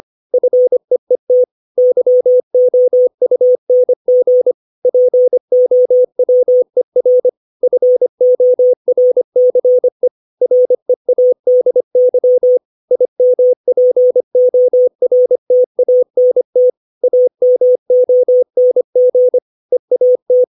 Morse code, heard as a keyed tone, about 25 wpm, 500 Hz. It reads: FEET YOUNG POWER FORCE READY IMPORTANT AMONG EAT
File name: 632 words / 632 Words 25WPM